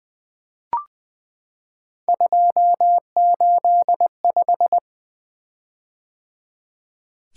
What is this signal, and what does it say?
Morse code, heard as a keyed tone, about 20 wpm, 700 Hz. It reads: 285
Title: Three Letter-Number Combinations - 20wpm